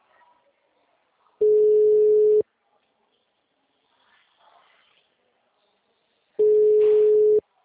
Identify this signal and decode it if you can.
Ringback tone (Europe, 425 Hz).
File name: out- -2002- - - .22